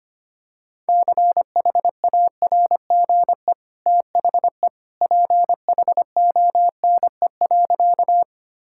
Morse code, heard as a keyed tone, about 25 wpm, 700 Hz. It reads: CHARGE THE PHONE.
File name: Easy Sentences with Hints - 25wpm